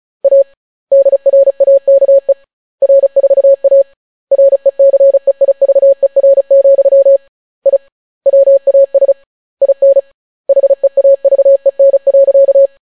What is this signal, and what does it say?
Morse code, heard as a keyed tone, about 36 wpm, 550 Hz. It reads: A DRAKE R4A RECEIVER, I WAS IN HEAVEN.